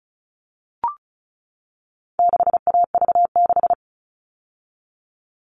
Morse code, heard as a keyed tone, about 34 wpm, 700 Hz. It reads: 6AV6
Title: All Cut Numbers - 4-Digit 35wpm